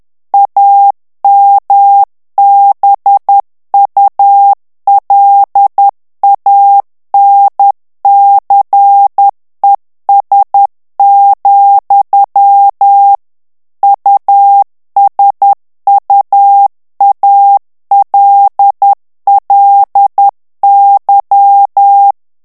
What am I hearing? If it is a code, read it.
Morse code, heard as a keyed tone, about 10 wpm, 800 Hz. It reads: AMBULANCES, USUALLY
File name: morse mp3 / ANGLAIS texte clair 2 vitesse 10 mots minute